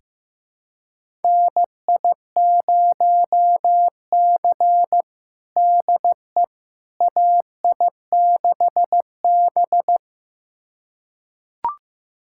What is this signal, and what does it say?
Morse code, heard as a keyed tone, about 15 wpm, 700 Hz. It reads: NI0C DE AI6B
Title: USA Callsign DE Callsign - 4 Characters - 15wpm